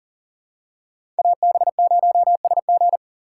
Morse code, heard as a keyed tone, about 40 wpm, 700 Hz. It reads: AB0SG
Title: USA Call Signs - 5 Characters - 3x Character Spacing - 40wpm